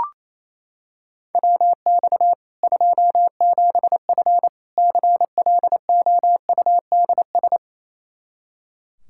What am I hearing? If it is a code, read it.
Morse code, heard as a keyed tone, about 28 wpm, 700 Hz. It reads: WX 27F CLOUDS
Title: Practice QSO - Exchange 2 - Info - 28wpm